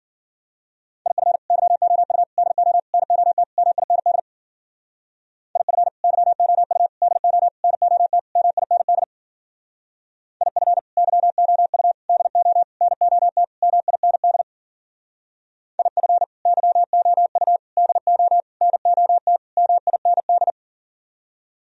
Morse code, heard as a keyed tone, about 50 wpm, 700 Hz. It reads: IF YOU DO NOT MIND IF YOU DO NOT MIND IF YOU DO NOT MIND IF YOU DO NOT MIND
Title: Speed-Racing - Sets of 5 Words 40wpm